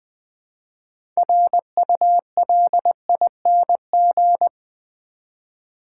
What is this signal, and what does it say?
Morse code, heard as a keyed tone, about 20 wpm, 700 Hz. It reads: RULING
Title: 6 Letter Words - First 3 Letters Given - 20wpm